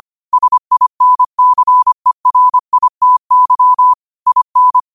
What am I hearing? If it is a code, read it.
Morse code, heard as a keyed tone, about 25 wpm, 1000 Hz. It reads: SINCERITY IN
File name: qotdfiles / 25wpm